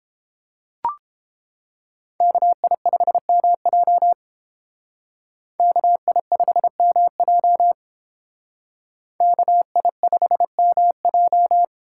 Morse code, heard as a keyed tone, about 31 wpm, 700 Hz. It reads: KI5MJ KI5MJ KI5MJ